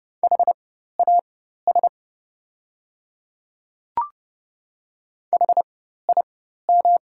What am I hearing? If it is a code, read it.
Morse code, heard as a keyed tone, about 30 wpm, spaced as for 14 wpm, 700 Hz. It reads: HAS HIM
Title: 4x Character Spacing - Top 100 Words - 30wpm